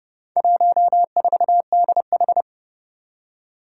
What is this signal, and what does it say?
Morse code, heard as a keyed tone, about 30 wpm, 700 Hz. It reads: 14DH
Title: Four Challenging Letter-Number Combinations 30wpm